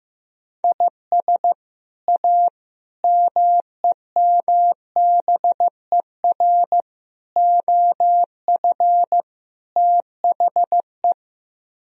Morse code, heard as a keyed tone, about 15 wpm, 700 Hz. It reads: IS A MEMBER OF THE
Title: Speed-Racing - Sets of 5 Words 15wpm